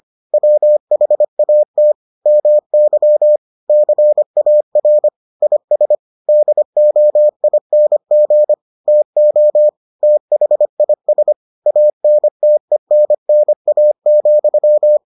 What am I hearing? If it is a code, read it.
Morse code, heard as a keyed tone, about 25 wpm, 600 Hz. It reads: WHAT MY CAR IS DOING TO THIS ANTENNA,